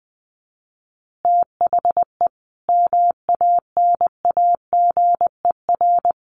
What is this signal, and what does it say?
Morse code, heard as a keyed tone, about 20 wpm, 700 Hz. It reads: THE MANAGER
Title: Additional Easy Phrases 20wpm